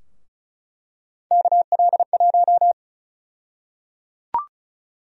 Morse code, heard as a keyed tone, about 34 wpm, 700 Hz. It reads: KL1
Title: USA Call Sign Prefixes - Repeated 2x - 35wpm